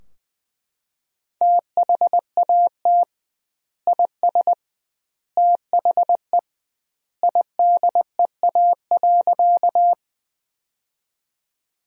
Morse code, heard as a keyed tone, about 21 wpm, 700 Hz. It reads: THAT IS THE IDEA.
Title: 2x Word-Spacing - Sentences from Top 300 Words - 20wpm